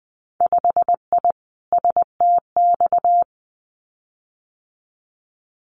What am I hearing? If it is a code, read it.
Morse code, heard as a keyed tone, about 19 wpm, 700 Hz. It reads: 5I STX
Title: Winter Field Day Exchanges Encore 20wpm